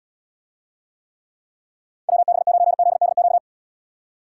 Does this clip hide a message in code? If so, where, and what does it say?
Yes: Morse code from 2.1 to 3.4 s, heard as a keyed tone, about 77 wpm, 700 Hz. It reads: WB9ZDQ